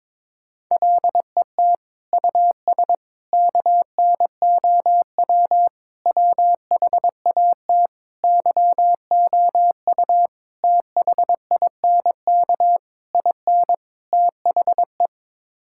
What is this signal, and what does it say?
Morse code, heard as a keyed tone, about 22 wpm, 700 Hz. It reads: LET US KNOW WHAT YOU THINK IN THE
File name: practice / 22wpm / Sets of 8 Words - 22wpm